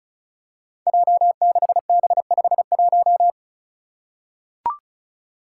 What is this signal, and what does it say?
Morse code, heard as a keyed tone, about 36 wpm, 700 Hz. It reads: J6B51
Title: Five Challenging Letter-Number Combinations 35wpm